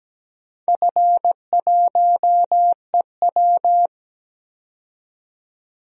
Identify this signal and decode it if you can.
Morse code, heard as a keyed tone, about 17 wpm, 700 Hz. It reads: F1EW
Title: DXCC-100 Call Signs - 4 Characters - 17wpm